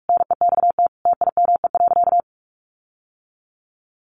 Morse code, heard as a keyed tone, about 44 wpm, 700 Hz. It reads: NEXT TIME.
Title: Sentences from Top 1500 Words 45wpm